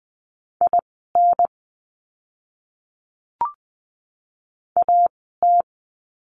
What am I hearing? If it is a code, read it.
Morse code, heard as a keyed tone, about 20 wpm, spaced as for 14 wpm, 700 Hz. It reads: IN AT
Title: 030 Lesson 6f - Letter i - Words 20wpm